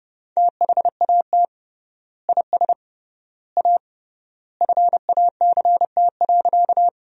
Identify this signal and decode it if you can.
Morse code, heard as a keyed tone, about 30 wpm, 700 Hz. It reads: THAT IS A FACT.
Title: 3x Word-Spacing - Sentences from Top 500 Words - 30wpm